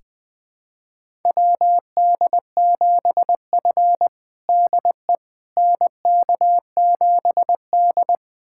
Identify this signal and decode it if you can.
Morse code, heard as a keyed tone, about 21 wpm, 700 Hz. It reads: WD7F DE NK7D
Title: USA Callsign DE Callsign - 4 Characters - 20wpm